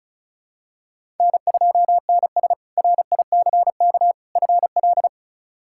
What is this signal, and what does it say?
Morse code, heard as a keyed tone, about 36 wpm, 700 Hz. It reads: N2NS RICK FL
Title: K1USN SST QSO Practice - 35wpm